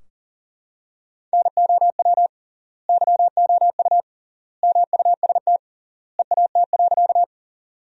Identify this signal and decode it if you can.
Morse code, heard as a keyed tone, about 37 wpm, 700 Hz. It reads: NOW YOU MUST EAT.
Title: 3x Word-Spacing - Sentences from Top 300 Words - 40wpm